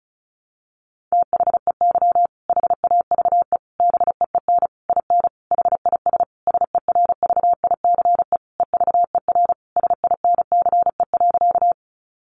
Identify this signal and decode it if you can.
Morse code, heard as a keyed tone, about 34 wpm, 700 Hz. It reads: THEY HAVE BEEN IN HIS SERVICE EVER SINCE.
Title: Sentences from Top 700 Words 35wpm